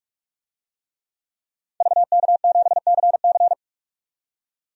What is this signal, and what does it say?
Morse code, heard as a keyed tone, about 45 wpm, 700 Hz. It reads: VK7CC
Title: DXCC-100 Call Signs - 5 Characters - 45wpm